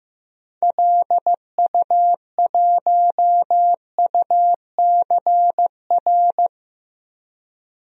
Morse code, heard as a keyed tone, about 15 wpm, 700 Hz. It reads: LU1UCR
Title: DXCC-100 Call Signs - 6 Characters - 15wpm